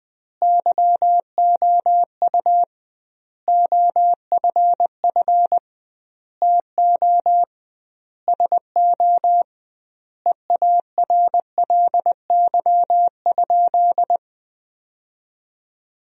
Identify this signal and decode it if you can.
Morse code, heard as a keyed tone, about 20 wpm, 700 Hz. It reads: YOU OFF TO SO EARLY?
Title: 2x Word-Spacing - Sentences from Top 400 Words - 20wpm